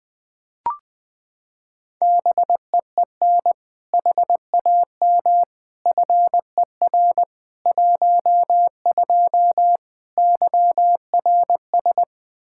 Morse code, heard as a keyed tone, about 20 wpm, 700 Hz. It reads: BEEN HAM FER 12 YRS
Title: Practice QSO - Exchange 3 - More Info - 20wpm